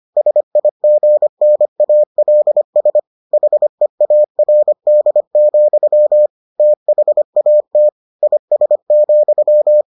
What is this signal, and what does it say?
Morse code, heard as a keyed tone, about 25 wpm, 600 Hz. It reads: SIGNALS HEARD, THAT IS,